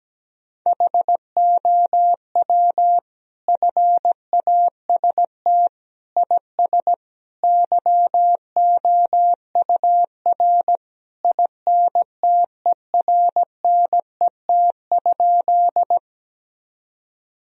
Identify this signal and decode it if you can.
Morse code, heard as a keyed tone, about 17 wpm, 700 Hz. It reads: HOW FAST IS YOUR INTERNET?
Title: Sentences from Top 1000 Words - Review Part 4 - 17wpm